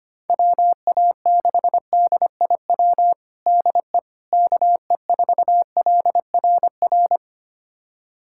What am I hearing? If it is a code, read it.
Morse code, heard as a keyed tone, about 25 wpm, 700 Hz. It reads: WA6DIW DE KE4LRR
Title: USA Callsign DE Callsign 25wpm